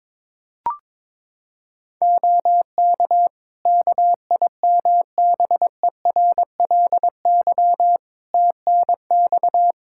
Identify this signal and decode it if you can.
Morse code, heard as a keyed tone, about 22 wpm, 700 Hz. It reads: OK KIMBERLY TNX
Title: Practice QSO - Exchange 4 - Ending - 22wpm